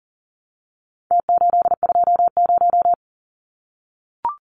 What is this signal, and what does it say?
Morse code, heard as a keyed tone, about 40 wpm, 700 Hz. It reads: T820
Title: Numbers - Contest Style - 4-Digit 40wpm